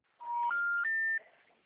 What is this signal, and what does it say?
Special information tone.
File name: out- -2013- - - .79156